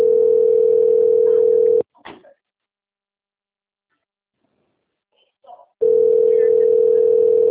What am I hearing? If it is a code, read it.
Ringback tone (North America), 440 + 480 Hz, with speech.
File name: out- - - - - .8934